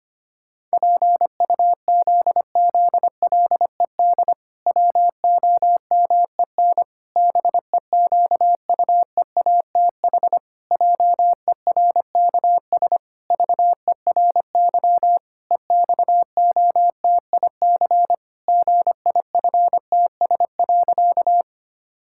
Morse code, heard as a keyed tone, about 25 wpm, 700 Hz. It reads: PUZZLED WOMEN BEQUEATH JERKS VERY EXOTIC GIFTS.
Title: Pangram Challenge v2 - 25wpm